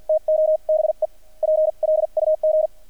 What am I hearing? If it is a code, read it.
Morse code, heard as a keyed tone, about 50 wpm, spaced as for 42 wpm, 620 Hz. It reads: TO BE PLUG
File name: archived / CRSnet